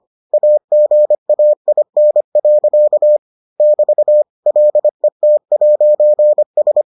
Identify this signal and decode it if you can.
Morse code, heard as a keyed tone, about 25 wpm, 600 Hz. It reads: AGAIN. <BT> LET'S